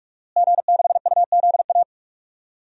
Morse code, heard as a keyed tone, about 46 wpm, 700 Hz. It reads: G6UZA
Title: DXCC-100 Call Signs - 5 Characters - Repeated 2x - 45wpm